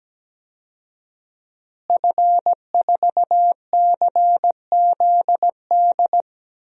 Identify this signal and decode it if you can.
Morse code, heard as a keyed tone, about 17 wpm, 700 Hz. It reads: F4CZD
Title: DXCC-100 Call Signs - 5 Characters - Repeated 2x - 17wpm